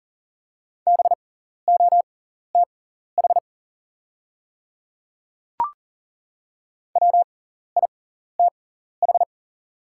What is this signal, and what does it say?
Morse code, heard as a keyed tone, about 40 wpm, spaced as for 14 wpm, 690 Hz. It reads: BOTH WITH
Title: 6x Character Spacing - Top 500 Words - 40wpm